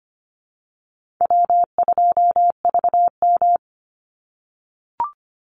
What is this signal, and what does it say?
Morse code, heard as a keyed tone, about 25 wpm, 700 Hz. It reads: W2VM
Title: Call Signs - 25wpm